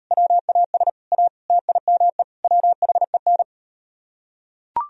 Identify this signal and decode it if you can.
Morse code, heard as a keyed tone, about 38 wpm, 700 Hz. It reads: WAS A TIME WHEN